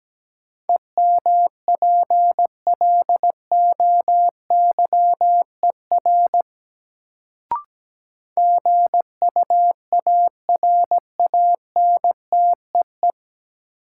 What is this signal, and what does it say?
Morse code, heard as a keyed tone, about 17 wpm, 700 Hz. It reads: EMPLOYER GUARANTEE
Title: Sentences from Top 1600 Words - New Word Drill - 17wpm